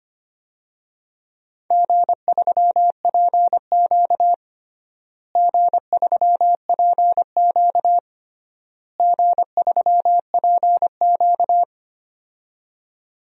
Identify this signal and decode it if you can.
Morse code, heard as a keyed tone, about 25 wpm, 700 Hz. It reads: G3PQ G3PQ G3PQ
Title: DXCC-100 Call Signs - 4 Characters - Repeated 3x - 25wpm